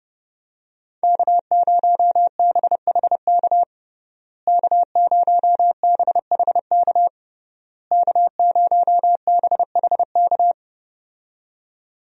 Morse code, heard as a keyed tone, about 32 wpm, 700 Hz. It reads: K0BHK K0BHK K0BHK
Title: USA Call Signs - 5 Characters - Repeated 3x - 30wpm